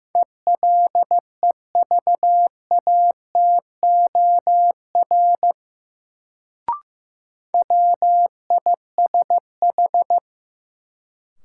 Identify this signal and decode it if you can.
Morse code, heard as a keyed tone, about 15 wpm, 700 Hz. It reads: ELEVATOR WISH
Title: Sentences from Top 1600 Words - New Word Drill - 15wpm